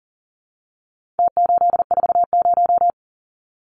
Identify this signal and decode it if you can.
Morse code, heard as a keyed tone, about 40 wpm, 700 Hz. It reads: T840
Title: Numbers - Contest Style - 4-Digit 40wpm